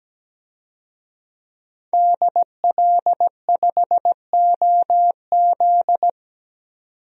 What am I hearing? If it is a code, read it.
Morse code, heard as a keyed tone, about 17 wpm, 700 Hz. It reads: DL5OZ
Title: DXCC-100 Call Signs - 5 Characters - 17wpm